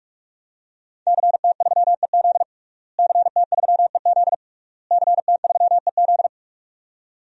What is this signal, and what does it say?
Morse code, heard as a keyed tone, about 46 wpm, 700 Hz. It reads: CT3EZ CT3EZ CT3EZ
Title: DXCC-100 Call Signs - 5 Characters - Repeated 3x - 45wpm